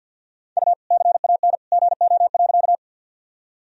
Morse code, heard as a keyed tone, about 53 wpm, 700 Hz. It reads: U CAN GO.
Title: Sentences from Top 100 Words and Abbreviations 50wpm